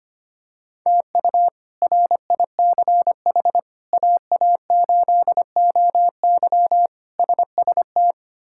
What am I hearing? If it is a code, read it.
Morse code, heard as a keyed tone, about 25 wpm, 700 Hz. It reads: TU RICH AA8OY SST
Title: K1USN SST QSO Practice - 25wpm